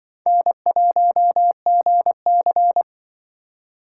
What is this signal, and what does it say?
Morse code, heard as a keyed tone, about 24 wpm, 700 Hz. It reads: N1GC
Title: Speed Racing - USA 1x2 Call Signs 20wpm